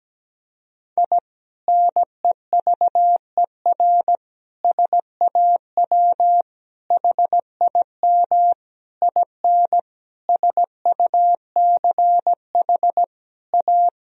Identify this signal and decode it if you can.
Morse code, heard as a keyed tone, about 17 wpm, 700 Hz. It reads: I NEVER SAW HIM IN SUCH A
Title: Sentences from Top 1000 Words - Review Part 2 - 17wpm